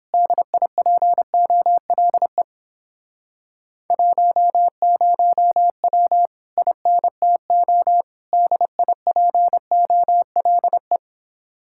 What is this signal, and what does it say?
Morse code, heard as a keyed tone, about 31 wpm, 700 Hz. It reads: DIPOLE 10W INTO DIPOLE